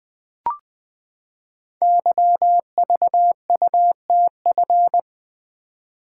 Morse code, heard as a keyed tone, about 20 wpm, 700 Hz. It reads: YVUTF
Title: Code Groups - 5 Letter:Numbers - 20wpm